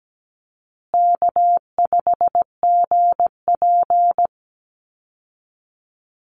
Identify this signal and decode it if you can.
Morse code, heard as a keyed tone, about 17 wpm, 700 Hz. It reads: K5GP
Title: Call Signs Repeated 2x - 17wpm